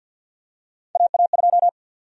Morse code, heard as a keyed tone, about 53 wpm, 700 Hz. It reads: AA2
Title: USA Call Sign Prefixes - 50wpm